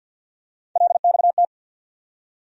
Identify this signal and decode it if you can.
Morse code, heard as a keyed tone, about 52 wpm, 700 Hz. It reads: LXT
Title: USA Call Sign Postfixes - 3 Letters 50wpm